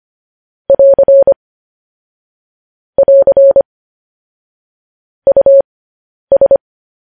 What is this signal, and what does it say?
Morse code, heard as a keyed tone, about 25 wpm, spaced as for 10 wpm, 550 Hz. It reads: <AR> <AR> US